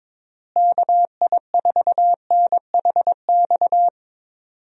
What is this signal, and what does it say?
Morse code, heard as a keyed tone, about 22 wpm, 700 Hz. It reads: KI4NHX
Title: USA Call Signs - 6 Characters - Repeated 2x - 22wpm